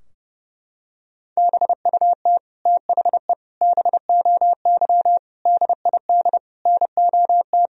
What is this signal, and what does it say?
Morse code, heard as a keyed tone, about 30 wpm, 700 Hz. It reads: BUT THE BOY DID NOT
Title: Sentences from Top 1000 Words - Review Part 3 - 30wpm